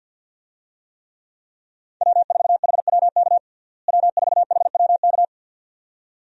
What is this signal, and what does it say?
Morse code, heard as a keyed tone, about 50 wpm, 700 Hz. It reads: W4HWK W4HWK
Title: USA Call Signs - 5 Characters - Repeated 2x - 50wpm